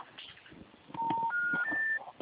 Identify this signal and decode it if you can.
Special information tone, with noise.
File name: out- -2036- - - .57